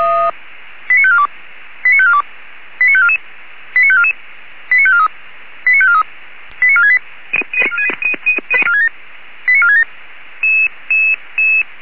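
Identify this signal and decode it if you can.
Five-tone selective call (ZVEI), heard as quick runs of five tones, tones of about 70 ms. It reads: 89542 89542 89544 89544 89543 89543 89468 89467
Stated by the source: ZVEI five-tone selective call received by SDR.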